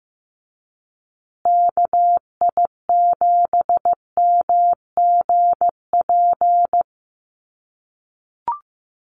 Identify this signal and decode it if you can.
Morse code, heard as a keyed tone, about 15 wpm, 700 Hz. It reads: KI7MGP
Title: USA Call Signs Repeated 2x 15wpm